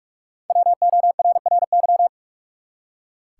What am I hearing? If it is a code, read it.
Morse code, heard as a keyed tone, about 44 wpm, 700 Hz. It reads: WORRY
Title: Five Letter Words - 45wpm